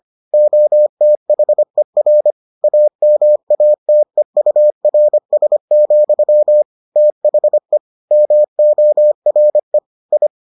Morse code, heard as a keyed tone, about 25 wpm, 600 Hz. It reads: OTHER AMATEURS, THE MORE I